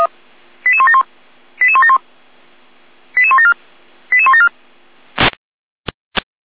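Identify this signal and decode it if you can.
Five-tone selective call (ZVEI), heard as quick runs of five tones, tones of about 80 ms. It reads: 88171 88171 88174 88174
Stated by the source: ZVEI five-tone selective call 88489 received via SDR.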